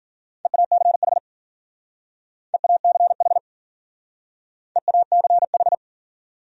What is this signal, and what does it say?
Morse code, heard as a keyed tone, about 51 wpm, 700 Hz. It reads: EACH EACH EACH